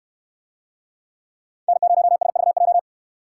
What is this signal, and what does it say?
Morse code, heard as a keyed tone, about 68 wpm, 700 Hz. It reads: N0SVJ